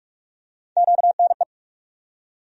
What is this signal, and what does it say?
Morse code, heard as a keyed tone, about 47 wpm, 700 Hz. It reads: QNE
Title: USA Call Sign Postfixes - 3 Letters 45wpm